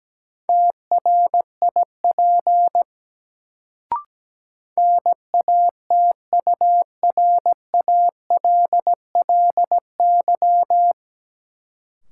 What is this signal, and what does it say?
Morse code, heard as a keyed tone, about 17 wpm, 700 Hz. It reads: TRIP NATURALLY
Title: Sentences from Top 1300 Words - New Word Drill Alt - 17wpm